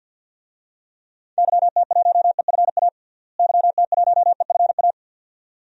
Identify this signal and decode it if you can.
Morse code, heard as a keyed tone, about 46 wpm, 700 Hz. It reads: YT1EFA YT1EFA
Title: DXCC-100 Call Signs - 6 Characters - Repeated 2x - 50wpm